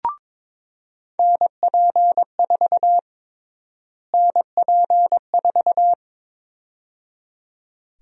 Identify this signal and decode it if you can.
Morse code, heard as a keyed tone, about 22 wpm, 700 Hz. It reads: NP4 NP4
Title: USA Call Sign Prefixes - Repeated 2x - 22wpm